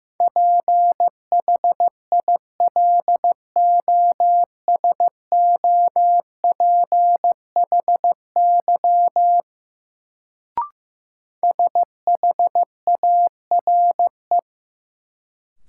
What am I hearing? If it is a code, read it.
Morse code, heard as a keyed tone, about 15 wpm, 700 Hz. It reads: PHILOSOPHY SHARE
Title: Sentences from Top 900 Words - New Word Drill - 15wpm